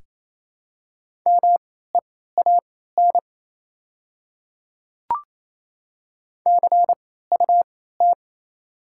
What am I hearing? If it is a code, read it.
Morse code, heard as a keyed tone, about 27 wpm, spaced as for 16 wpm, 700 Hz. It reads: MEAN CUT
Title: 3x Character Spacing - Top 300 Words - 28wpm